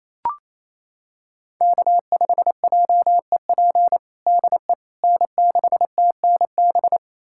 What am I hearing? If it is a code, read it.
Morse code, heard as a keyed tone, about 28 wpm, 700 Hz. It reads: K5JEP DE N6TNB
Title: USA Callsign DE Callsign - 5 Characters - 28wpm